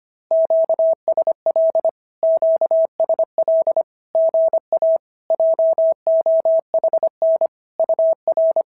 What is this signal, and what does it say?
Morse code, heard as a keyed tone, about 26 wpm, 650 Hz. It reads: QSL QSL GA JOHN UR
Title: SOTA QSO Practice 25wpm